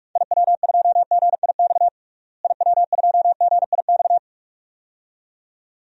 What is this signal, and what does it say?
Morse code, heard as a keyed tone, about 45 wpm, 700 Hz. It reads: IW2GIX IW2GIX
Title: DXCC-100 Call Signs - 6 Characters - Repeated 2x - 45wpm